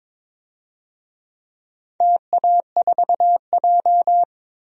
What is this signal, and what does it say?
Morse code, heard as a keyed tone, about 22 wpm, 700 Hz. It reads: TA4J
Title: DXCC-100 Call Signs - 4 Characters - 22wpm